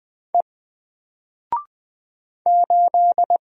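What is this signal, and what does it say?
Morse code, heard as a keyed tone, about 20 wpm, 700 Hz. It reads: E 8
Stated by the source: Single letter and number Morse drill at 20 wpm, from N4LCD.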